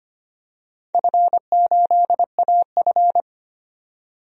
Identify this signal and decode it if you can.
Morse code, heard as a keyed tone, about 25 wpm, 700 Hz. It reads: F8AF
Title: DXCC-100 Call Signs - 4 Characters - 25wpm